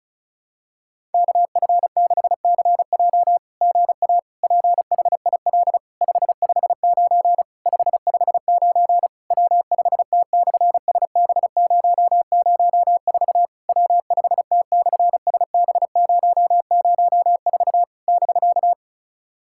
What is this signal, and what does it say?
Morse code, heard as a keyed tone, about 36 wpm, 700 Hz. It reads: KF6CJ GA PHIL 559 559 W5T/SB004 W5T/SB004 <BK>
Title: SOTA QSO Practice 35wpm